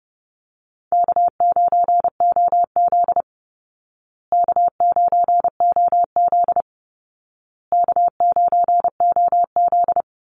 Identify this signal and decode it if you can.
Morse code, heard as a keyed tone, about 30 wpm, 700 Hz. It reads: K9OZ K9OZ K9OZ
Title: Easy Call Signs Repeated 3x - 30wpm